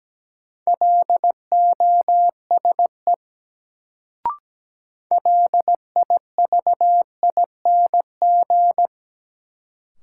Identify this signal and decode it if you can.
Morse code, heard as a keyed tone, about 17 wpm, 700 Hz. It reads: LOSE LIVING
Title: Sentences from Top 1100 Words - New Word Drill Alt - 17wpm